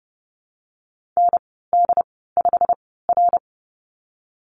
Morse code, heard as a keyed tone, about 30 wpm, spaced as for 17 wpm, 700 Hz. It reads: ND5R
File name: course / 30:10wpm / 088 Lesson 15g - Number 5 - Call Signs 30wpm